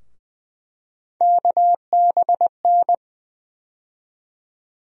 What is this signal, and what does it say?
Morse code, heard as a keyed tone, about 20 wpm, 700 Hz. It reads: KBN
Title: USA Call Sign Postfixes - 3 Letters 20wpm